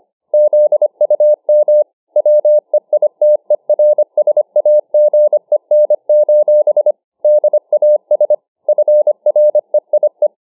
Morse code, heard as a keyed tone, about 25 wpm, 600 Hz. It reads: ZUM WEITERSAGEN: DAS FREIE